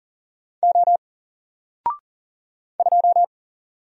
Morse code, heard as a keyed tone, about 39 wpm, 700 Hz. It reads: O 2
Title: Single Character (Letter:Number:Punctuation:Prosign) 40wpm